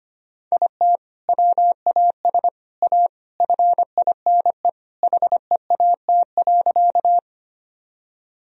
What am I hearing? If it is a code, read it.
Morse code, heard as a keyed tone, about 25 wpm, 700 Hz. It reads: IT WAS A FINE HEAT.
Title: Sentences from Top 1000 Words - Review Part 3 - 25wpm